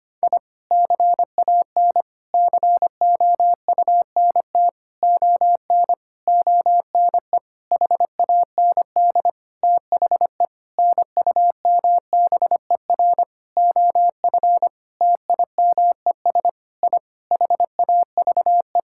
Morse code, heard as a keyed tone, about 25 wpm, 700 Hz. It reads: I CAN COUNT ON ONE HAND THE NUMBER OF TIMES I HAVE
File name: practice / 25wpm / Sentences with 9-Word Phrases - 25wpm